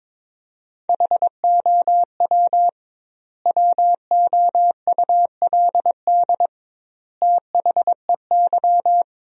Morse code, heard as a keyed tone, about 21 wpm, 700 Hz. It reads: HOW WOULD THEY
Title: 2x Word-Spacing - Sentences from Top 300 Words - 22wpm